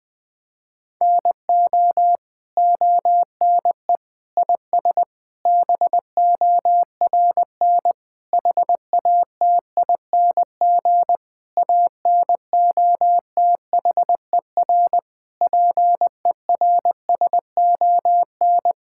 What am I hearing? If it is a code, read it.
Morse code, heard as a keyed tone, about 20 wpm, 700 Hz. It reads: NO ONE IS BORN HATING ANOTHER PERSON
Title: Sets of 9 Words - 20wpm